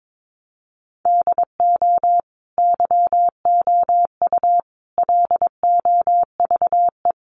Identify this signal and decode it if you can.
Morse code, heard as a keyed tone, about 22 wpm, 700 Hz. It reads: DO YOU LOVE
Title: Sentences from Top 400 Words 22wpm